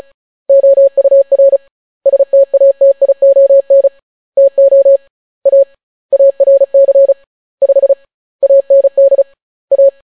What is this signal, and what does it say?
Morse code, heard as a keyed tone, about 36 wpm, 550 Hz. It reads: OUR STATION TO A ARC 5 AND A